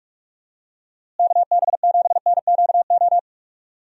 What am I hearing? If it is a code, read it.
Morse code, heard as a keyed tone, about 41 wpm, 700 Hz. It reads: KB7NQO
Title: USA Call Signs - 6 Characters - Repeated 3x - 45wpm